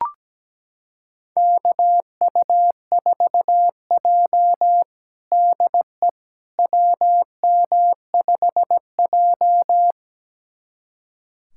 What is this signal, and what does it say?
Morse code, heard as a keyed tone, about 17 wpm, 700 Hz. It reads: KU4J DE WM5J
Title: USA Callsign DE Callsign - 4 Characters - 17wpm